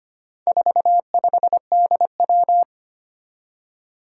Morse code, heard as a keyed tone, about 25 wpm, 700 Hz. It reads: 45DW
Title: Four Challenging Letter-Number Combinations 25wpm